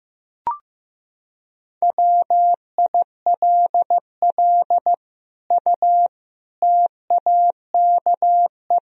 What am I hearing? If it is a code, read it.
Morse code, heard as a keyed tone, about 15 wpm, 700 Hz. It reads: WILL U TAKE
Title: Sentences from Top 200 Words and Abbreviations 15wpm